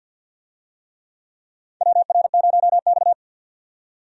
Morse code, heard as a keyed tone, about 48 wpm, 700 Hz. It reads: WR0X
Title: USA Call Signs - 4 Characters - 50wpm